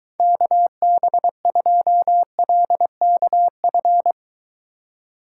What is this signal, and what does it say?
Morse code, heard as a keyed tone, about 22 wpm, 700 Hz. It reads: KB2LKF